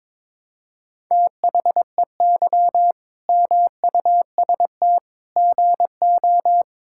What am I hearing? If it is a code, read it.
Morse code, heard as a keyed tone, about 21 wpm, 700 Hz. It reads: THEY MUST GO
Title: Sentences from Top 1600 Words 22wpm